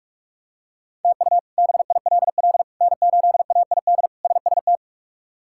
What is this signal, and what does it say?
Morse code, heard as a keyed tone, about 46 wpm, 700 Hz. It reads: TU BILL N8AID SST
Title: K1USN SST QSO Practice - 45wpm